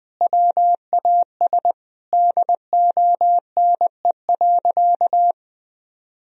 Morse code, heard as a keyed tone, about 21 wpm, 700 Hz. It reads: WAS DONE.
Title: Sentences from Top 1000 Words - Review Part 1 - 20wpm